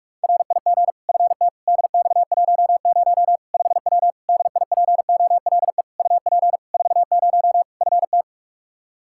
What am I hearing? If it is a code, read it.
Morse code, heard as a keyed tone, about 45 wpm, 700 Hz. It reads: RIG FT DX10 5W DIPOLE UP 40 FT
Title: Practice QSO - Exchange 2 - Info - 45wpm